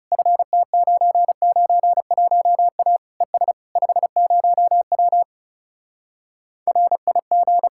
Morse code, heard as a keyed tone, about 40 wpm, 700 Hz. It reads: FT991A ES 50W RIG